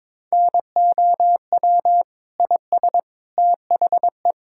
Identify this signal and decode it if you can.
Morse code, heard as a keyed tone, about 22 wpm, 700 Hz. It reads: NOW IS THE
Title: Pangram Challenge v2 - 22wpm